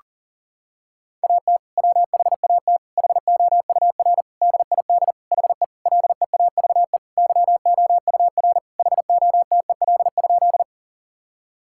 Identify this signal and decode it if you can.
Morse code, heard as a keyed tone, about 40 wpm, 700 Hz. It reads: AT WHAT HOUR DID HE LEAVE YOUR HOTEL?
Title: Sentences from Top 1600 Words 40wpm